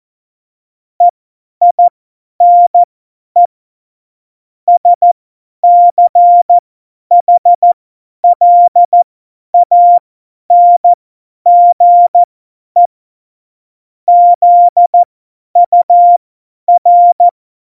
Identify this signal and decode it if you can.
Morse code, heard as a keyed tone, about 14 wpm, spaced as for 10 wpm, 700 Hz. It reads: EINE SCHLANGE ZUR
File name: Morsekurs / Witze / Witz 17 April 2018, 50 F70